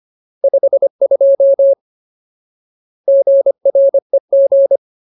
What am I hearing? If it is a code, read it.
Morse code, heard as a keyed tone, about 25 wpm, 550 Hz. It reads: 52 GREG
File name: Ultimate Word List / 25WPM Ultimate